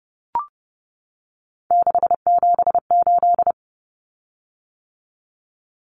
Morse code, heard as a keyed tone, about 30 wpm, 700 Hz. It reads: 678
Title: Common Cut Numbers - 3-Digit 30wpm